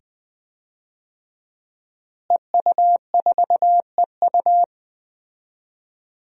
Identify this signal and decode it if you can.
Morse code, heard as a keyed tone, about 21 wpm, 700 Hz. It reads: EU4EU
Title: DXCC-100 Call Signs - 5 Characters - Repeated 2x - 20wpm